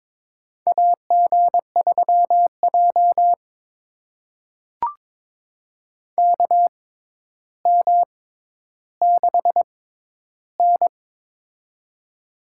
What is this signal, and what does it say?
Morse code, heard as a keyed tone, about 21 wpm, spaced as for 11 wpm, 700 Hz. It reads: AG3J KM6N
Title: USA Call Signs - 4 Characters - 6x Character Spacing - 22wpm